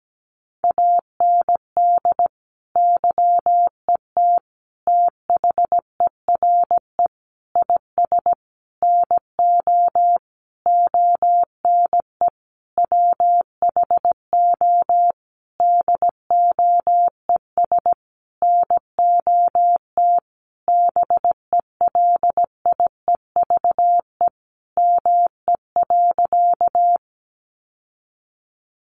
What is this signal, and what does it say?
Morse code, heard as a keyed tone, about 17 wpm, 700 Hz. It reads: AND YET THERE IS NO ONE WHO DOES NOT BELIEVE ME.
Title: Sentences from Top 1000 Words 17wpm